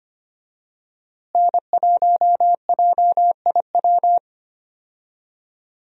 Morse code, heard as a keyed tone, about 25 wpm, 700 Hz. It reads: N1JIW
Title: USA Call Signs - 5 Characters - Repeated 3x - 25wpm